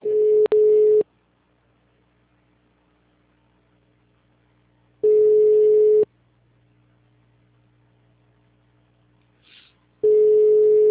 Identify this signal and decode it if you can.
Ringback tone (Europe, 425 Hz).